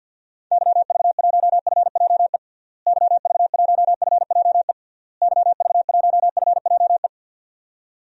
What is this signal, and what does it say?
Morse code, heard as a keyed tone, about 52 wpm, 700 Hz. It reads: YV1FJE YV1FJE YV1FJE
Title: DXCC-100 Call Signs - 6 Characters - Repeated 3x - 50wpm